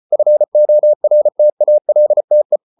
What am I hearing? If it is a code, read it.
Morse code, heard as a keyed tone, about 34 wpm, 600 Hz. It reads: FORTALTE
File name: Ferie QRQ 34wpm